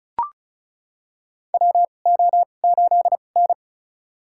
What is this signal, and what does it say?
Morse code, heard as a keyed tone, about 36 wpm, spaced as for 25 wpm, 700 Hz. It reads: WO8N
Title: USA Call Signs - 2x Character Spacing - 35wpm